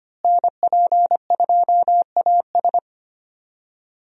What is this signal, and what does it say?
Morse code, heard as a keyed tone, about 25 wpm, 700 Hz. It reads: NP2AS
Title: USA Call Signs - 5 Characters - Repeated 3x - 25wpm